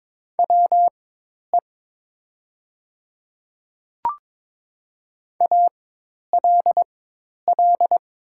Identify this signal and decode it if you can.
Morse code, heard as a keyed tone, about 22 wpm, spaced as for 10 wpm, 700 Hz. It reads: WE ALL
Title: 4x Character Spacing - Top 300 Words - 22wpm